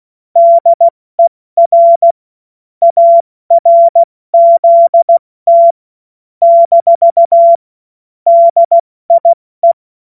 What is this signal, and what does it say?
Morse code, heard as a keyed tone, about 16 wpm, 680 Hz. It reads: DER ARZT - DIE
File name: Morsekurs / Witze / Witz 21.November 2023 70 F80